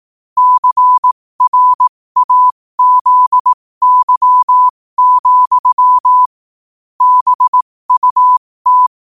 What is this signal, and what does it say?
Morse code, heard as a keyed tone, about 18 wpm, spaced as for 15 wpm, 1000 Hz. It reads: CRAZY, BUT
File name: qotdfiles / 15wpm